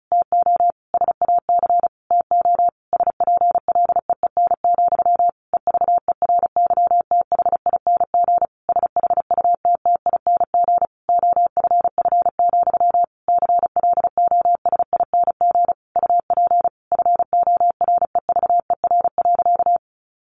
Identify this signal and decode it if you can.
Morse code, heard as a keyed tone, about 36 wpm, 700 Hz. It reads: TO SAC TO SPLEEN, EVERYTHING SHUTTING OFF, CLOSING UP FOREVER.
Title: A Sound Of Thunder 35wpm